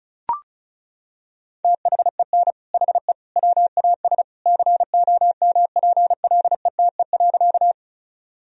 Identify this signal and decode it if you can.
Morse code, heard as a keyed tone, about 36 wpm, 700 Hz. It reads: THEN HE WAS COMPLETE.
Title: Sentences from Top 1000 Words - Review Part 2 - 35wpm